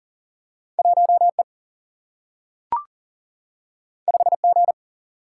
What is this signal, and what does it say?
Morse code, heard as a keyed tone, about 40 wpm, 700 Hz. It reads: 1 E 5 G
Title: Single-Letter Single-Number 40wpm